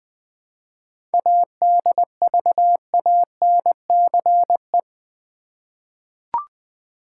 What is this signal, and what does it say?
Morse code, heard as a keyed tone, about 20 wpm, 700 Hz. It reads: ADVANCE
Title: 7 Letter Words - First 3 Letters Given - 20wpm